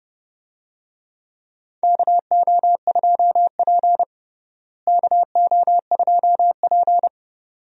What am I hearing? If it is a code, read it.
Morse code, heard as a keyed tone, about 30 wpm, 700 Hz. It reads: KO2P KO2P
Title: USA Call Signs - 4 Characters - Repeated 2x - 30wpm